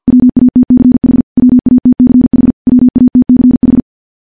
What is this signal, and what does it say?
Morse code, heard as a keyed tone, about 50 wpm, 250 Hz. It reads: WATCH WATCH WATCH
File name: morse / morse-training-50WPM